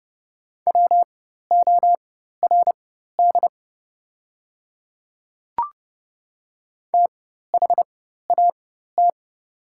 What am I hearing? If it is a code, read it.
Morse code, heard as a keyed tone, about 30 wpm, spaced as for 14 wpm, 700 Hz. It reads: WORD THAT
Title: 4x Character Spacing - Top 200 Words - 30wpm